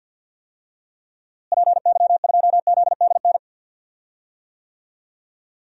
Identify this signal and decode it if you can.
Morse code, heard as a keyed tone, about 48 wpm, 700 Hz. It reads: PY2ZDN
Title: DXCC-100 Call Signs - 6 Characters - Repeated 3x - 50wpm